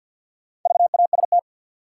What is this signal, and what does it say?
Morse code, heard as a keyed tone, about 52 wpm, 700 Hz. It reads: VAST
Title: Sets of 4 Words - New Word Drill - 50wpm